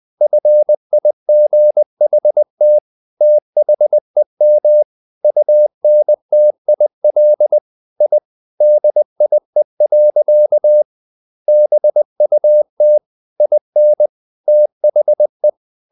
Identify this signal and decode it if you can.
Morse code, heard as a keyed tone, about 20 wpm, 600 Hz. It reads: FIGHT THEM UNTIL I DIE. BUT IN THE
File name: the old man and the sea part 2 20wpm